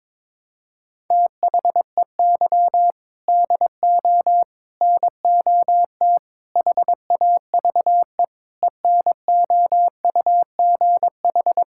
Morse code, heard as a keyed tone, about 22 wpm, 700 Hz. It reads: THEY DO NOT HAVE ENOUGH
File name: practice / 22wpm / Sentences with 4-Word Phrases - 22wpm